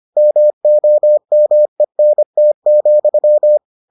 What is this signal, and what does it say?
Morse code, heard as a keyed tone, about 25 wpm, 600 Hz. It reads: MOMENT,